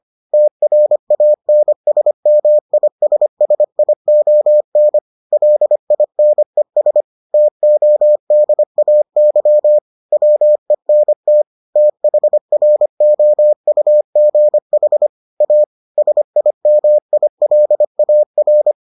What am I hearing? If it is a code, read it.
Morse code, heard as a keyed tone, about 25 wpm, 600 Hz. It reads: TRANSMISSION LINES TODAY WENT THROUGH A SIMILAR